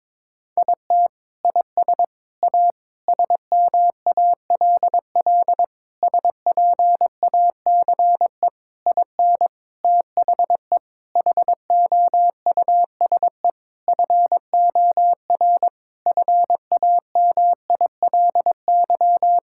Morse code, heard as a keyed tone, about 22 wpm, 700 Hz. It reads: IT IS A SMALL SPACE IN THE HOUSE FOR FAMILY
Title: Sentences from Top 1000 Words - Review Part 5 - 22wpm